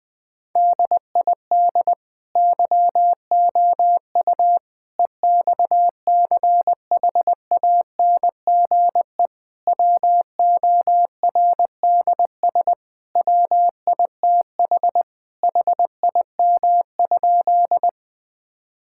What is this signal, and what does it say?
Morse code, heard as a keyed tone, about 21 wpm, 700 Hz. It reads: DID YOU EXCHANGE WORDS WITH HIM?
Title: Sentences from Top 1000 Words - Review Part 6 - 20wpm